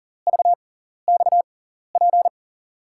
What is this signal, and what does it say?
Morse code, heard as a keyed tone, about 40 wpm, spaced as for 14 wpm, 700 Hz. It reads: VXP
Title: USA Call Signs - 5 Characters - 6x Character Spacing - 40wpm